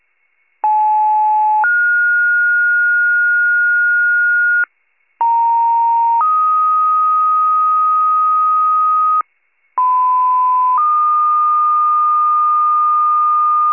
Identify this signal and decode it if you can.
Two-tone page, heard as a one-second tone, then a three-second tone, about 860 then 1430 Hz, about 930 then 1250 Hz, about 1010 then 1220 Hz. It is heard from 0.6 to 13.7 s.
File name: pager / 94 RESCUE / wcalert